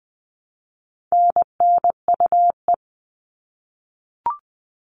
Morse code, heard as a keyed tone, about 20 wpm, 700 Hz. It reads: NNUE
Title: All Cut Numbers - 4-Digit 20wpm